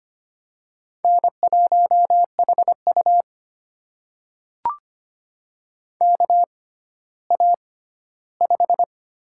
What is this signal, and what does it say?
Morse code, heard as a keyed tone, about 25 wpm, spaced as for 14 wpm, 700 Hz. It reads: N1HU KA5
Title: USA Call Signs - 6x Character Spacing - 25wpm